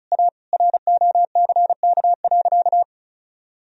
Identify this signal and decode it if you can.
Morse code, heard as a keyed tone, about 36 wpm, 700 Hz. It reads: A ROCK.
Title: Sentences from Top 1000 Words - Review Part 2 - 35wpm